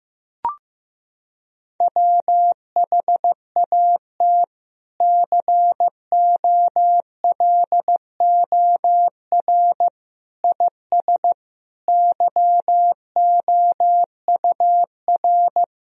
Morse code, heard as a keyed tone, about 16 wpm, 700 Hz. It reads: WHAT COLOR IS YOUR
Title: Sentences from Top 1000 Words - Review Part 3 - 15wpm